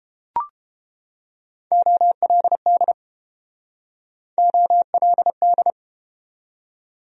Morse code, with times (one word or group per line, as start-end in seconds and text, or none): 1.7-2.9 OLD
4.4-5.7 OLD